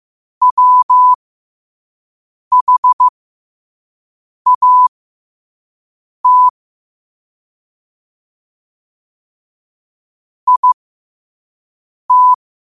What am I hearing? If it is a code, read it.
Morse code, heard as a keyed tone, about 15 wpm, spaced as for 5 wpm, 1000 Hz. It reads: WHAT IT